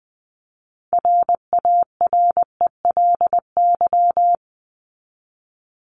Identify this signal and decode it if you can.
Morse code, heard as a keyed tone, about 20 wpm, 700 Hz. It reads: RARELY
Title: LY Suffix - Top 100 Words with Base Words 20wpm